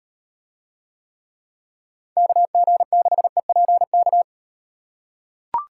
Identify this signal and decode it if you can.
Morse code, heard as a keyed tone, about 35 wpm, 700 Hz. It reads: KG6EPK